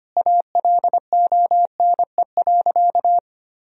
Morse code, heard as a keyed tone, about 25 wpm, 700 Hz. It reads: ALONE.
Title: Sentences from Top 1000 Words - Review Part 4 - 25wpm